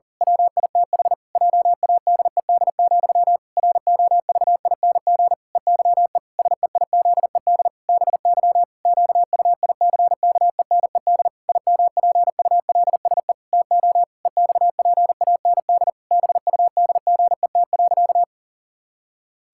Morse code, heard as a keyed tone, about 40 wpm, 700 Hz. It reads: WITH JADED, ROVING EYE SEIZED BY QUICKENED IMPULSE TO EXPAND BUDGET.
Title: Pangram Challenge v2 - 40wpm